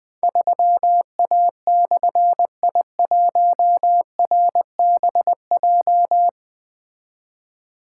Morse code, heard as a keyed tone, about 20 wpm, 700 Hz. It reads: 3A/I1RBJ
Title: Difficult Call Signs 20wpm